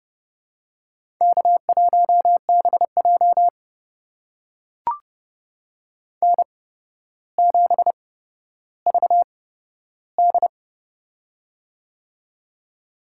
Morse code, heard as a keyed tone, about 28 wpm, spaced as for 13 wpm, 700 Hz. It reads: K1BJ N7VD
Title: USA Call Signs - 4 Characters - 8x Character Spacing - 30wpm